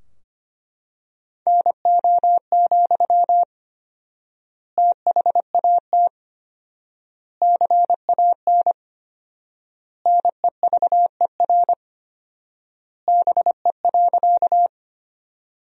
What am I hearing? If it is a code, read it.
Morse code, heard as a keyed tone, about 25 wpm, 700 Hz. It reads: NO, THAT CAN NEVER BE.
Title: 4x Word-Spacing - Sentences from Top 300 Words - 25wpm